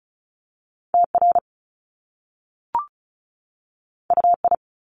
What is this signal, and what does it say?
Morse code, heard as a keyed tone, about 36 wpm, 700 Hz. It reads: TR UI
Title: Top Two Letter Combinations in Words 35wpm